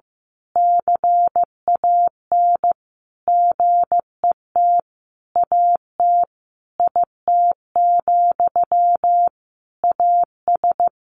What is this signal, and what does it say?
Morse code, heard as a keyed tone, about 14 wpm, 700 Hz. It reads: CAN GET AT IT, AS
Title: A Sound Of Thunder 15wpm